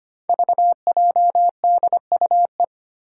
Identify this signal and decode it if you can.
Morse code, heard as a keyed tone, about 25 wpm, 700 Hz. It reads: VJDUE
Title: Five Challenging Letter-Number Combinations 25wpm